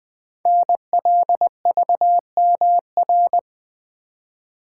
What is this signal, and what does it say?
Morse code, heard as a keyed tone, about 20 wpm, 700 Hz. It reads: NLVMR
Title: Code Groups - 5 Letters - 20wpm